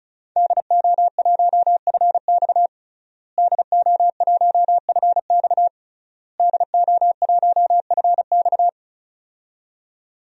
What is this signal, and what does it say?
Morse code, heard as a keyed tone, about 36 wpm, 700 Hz. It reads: DO1FX DO1FX DO1FX
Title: DXCC-100 Call Signs - 5 Characters - Repeated 3x - 35wpm